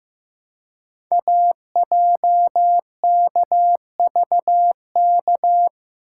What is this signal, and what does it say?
Morse code, heard as a keyed tone, about 15 wpm, 700 Hz. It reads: AJKVK
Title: Code Groups - 5 Letters - 15wpm